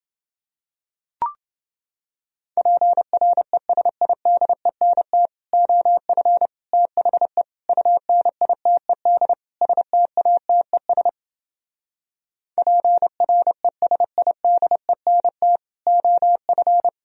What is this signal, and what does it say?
Morse code, heard as a keyed tone, about 30 wpm, 700 Hz. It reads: PRESIDENT OF THE UNITED STATES PRESIDENT OF